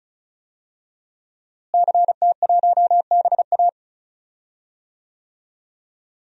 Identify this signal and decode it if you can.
Morse code, heard as a keyed tone, about 36 wpm, 700 Hz. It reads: CT1BA
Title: DXCC-100 Call Signs - 5 Characters - Repeated 3x - 35wpm